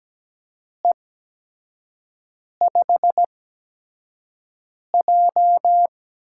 Morse code, heard as a keyed tone, about 17 wpm, spaced as for 5 wpm, 700 Hz. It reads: E5J
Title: USA Call Signs - 4 Characters - 8x Character Spacing - 17wpm